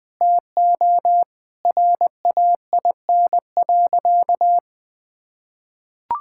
Morse code, heard as a keyed tone, about 21 wpm, 700 Hz. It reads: TO RAIN.
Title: Sentences with 5-Word Phrases - 20wpm